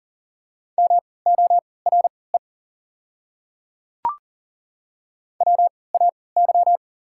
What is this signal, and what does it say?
Morse code, heard as a keyed tone, about 40 wpm, spaced as for 23 wpm, 690 Hz. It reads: MORE WAY
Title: 3x Character Spacing - Top 200 Words - 40wpm